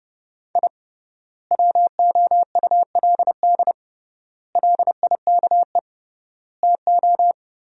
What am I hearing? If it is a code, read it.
Morse code, heard as a keyed tone, about 30 wpm, 700 Hz. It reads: I WOULD LIKE TO
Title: 3x Word-Spacing - Sentences from Top 200 Words - 30wpm